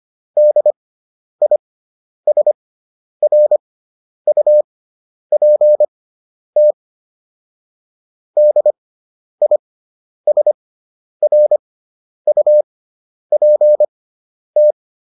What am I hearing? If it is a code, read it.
Morse code, heard as a keyed tone, about 25 wpm, spaced as for 10 wpm, 600 Hz. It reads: DISRUPT DISRUPT